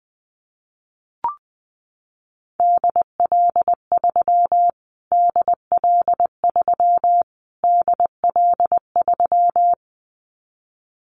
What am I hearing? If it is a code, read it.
Morse code, heard as a keyed tone, about 20 wpm, 700 Hz. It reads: DL3 DL3 DL3
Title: Callsign Prefixes for Top 25 DXCC - Repeated 3x - 20wpm